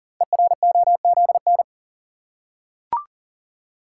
Morse code, heard as a keyed tone, about 42 wpm, 700 Hz. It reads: EROZN
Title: Code Groups - 5 Letters - 40wpm